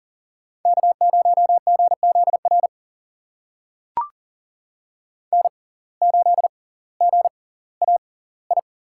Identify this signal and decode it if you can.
Morse code, heard as a keyed tone, about 38 wpm, spaced as for 20 wpm, 700 Hz. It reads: K0GZR N8GAI
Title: USA Call Signs - 5 Characters - 6x Character Spacing - 40wpm